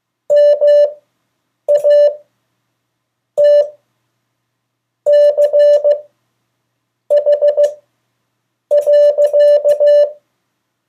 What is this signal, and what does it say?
Morse code, heard as a keyed tone, about 15 wpm, spaced as for 6 wpm, 580 Hz. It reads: MATCH.
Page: CW Class Review – Delaware Valley Radio Association